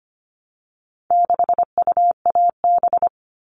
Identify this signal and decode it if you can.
Morse code, heard as a keyed tone, about 25 wpm, 700 Hz. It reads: 6UAB
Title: All Cut Numbers - 4-Digit 25wpm